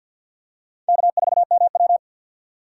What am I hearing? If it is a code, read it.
Morse code, heard as a keyed tone, about 50 wpm, 700 Hz. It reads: K4MW
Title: USA Call Signs - 6x Character Spacing - 50wpm